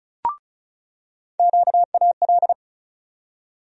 Morse code, heard as a keyed tone, about 34 wpm, 700 Hz. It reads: QAL
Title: USA Call Sign Postfixes - 3 Letters 35wpm